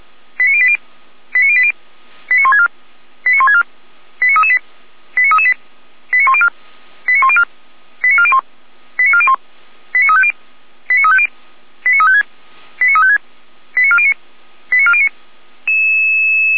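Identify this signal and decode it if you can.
Five-tone selective call (ZVEI), heard as quick runs of five tones, tones of about 70 ms. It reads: 89080 89080 89164 89164 89338 89338 89104 89104 89401 89401 89350 89350 89356 89356 89409 89409
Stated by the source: ZVEI five-tone call 89235 received on an SDR, from the fire brigade server of Feuerwehr Orlamünde.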